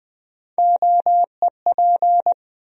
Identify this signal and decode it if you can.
Morse code, heard as a keyed tone, about 20 wpm, 700 Hz. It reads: OEP
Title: USA Call Sign Postfixes - 3 Letters 20wpm